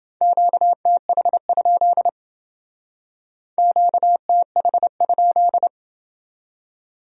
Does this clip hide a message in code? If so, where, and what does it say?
Yes: Morse code from 0.2 to 5.7 s, heard as a keyed tone, about 28 wpm, 700 Hz. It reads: QTH? QTH?